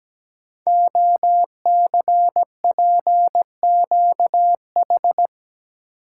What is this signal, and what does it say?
Morse code, heard as a keyed tone, about 17 wpm, 700 Hz. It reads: OCPQH
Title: Code Groups - 5 Letter:Numbers - 17wpm